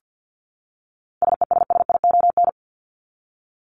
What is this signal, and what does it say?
Morse code, heard as a keyed tone, about 50 wpm, 700 Hz. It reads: SESSION
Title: Top 1501-1750 Words 50wpm